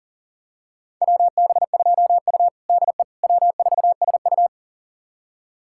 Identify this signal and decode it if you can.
Morse code, heard as a keyed tone, about 40 wpm, 700 Hz. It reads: WB2U DE W4SU
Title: USA Callsign DE Callsign - 4 Characters - 40wpm